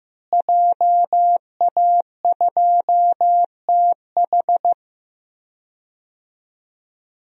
Morse code, heard as a keyed tone, about 16 wpm, 700 Hz. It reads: JA2TH
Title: DXCC-100 Call Signs - 5 Characters - Repeated 3x - 15wpm